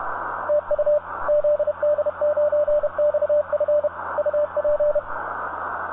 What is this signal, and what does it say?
Morse code, heard as a keyed tone, about 31 wpm, 590 Hz, with radio noise. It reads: TU ZD9XF UP
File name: zd9xf-on-24mhz-12-sept-2014-13-04utc-cw